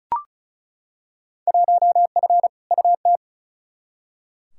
Morse code, heard as a keyed tone, about 34 wpm, 690 Hz. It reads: 1F UT
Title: Field Day Exchanges Encore - 35wpm